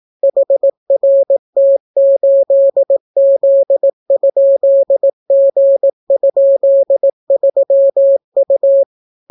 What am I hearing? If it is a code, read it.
Morse code, heard as a keyed tone, about 18 wpm, 550 Hz. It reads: HRT8Z?G?3U